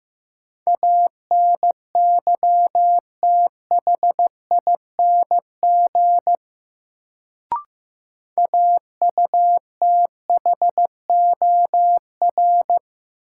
Morse code, heard as a keyed tone, about 15 wpm, 700 Hz. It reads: ANYTHING AUTHOR
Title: Sets of 4 Words - New Word Drill - 15wpm